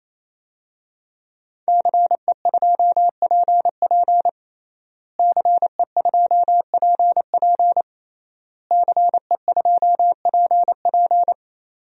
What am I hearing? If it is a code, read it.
Morse code, heard as a keyed tone, about 28 wpm, 700 Hz. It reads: CE2PP CE2PP CE2PP
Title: DXCC-100 Call Signs - 5 Characters - Repeated 3x - 28wpm